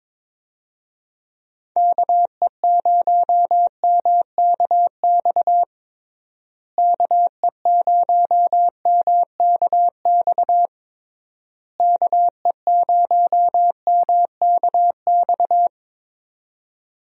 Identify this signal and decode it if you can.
Morse code, heard as a keyed tone, about 22 wpm, 700 Hz. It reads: KE0MKX KE0MKX KE0MKX
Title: USA Call Signs - 6 Characters - Repeated 3x - 22wpm